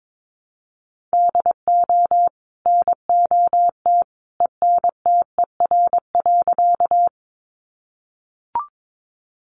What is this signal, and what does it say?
Morse code, heard as a keyed tone, about 22 wpm, 700 Hz. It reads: DO NOT ENTER.
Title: Sentences from Top 1300 Words 22wpm